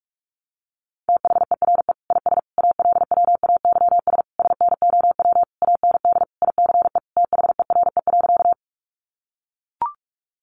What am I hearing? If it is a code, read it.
Morse code, heard as a keyed tone, about 44 wpm, 700 Hz. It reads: THERE IS ALWAYS SNOW AND ICE THERE.
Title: Sentences from Top 1000 Words 45wpm